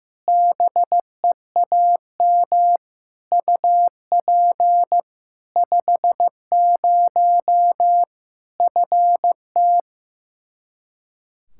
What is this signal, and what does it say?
Morse code, heard as a keyed tone, about 15 wpm, 700 Hz. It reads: BEAM UP 50 FT
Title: K1USN SST QSO Practice 15wpm